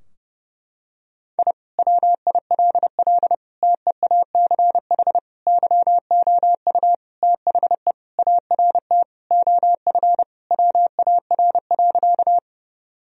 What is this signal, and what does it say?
Morse code, heard as a keyed tone, about 32 wpm, 700 Hz. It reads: I WILL TEACH YOU THE ART OF WAR.
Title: Sentences from Top 1000 Words - Review Part 3 - 30wpm